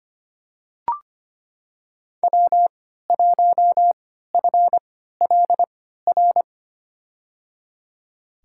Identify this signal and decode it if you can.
Morse code, heard as a keyed tone, about 25 wpm, spaced as for 14 wpm, 700 Hz. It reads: W1FLR
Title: USA Call Signs - 5 Characters - 3x Character Spacing - 25wpm